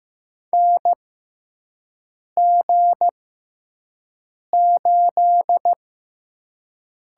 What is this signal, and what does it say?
Morse code, heard as a keyed tone, about 15 wpm, spaced as for 5 wpm, 700 Hz. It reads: NG8
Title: USA Call Signs - 4 Characters - 6x Character Spacing - 15wpm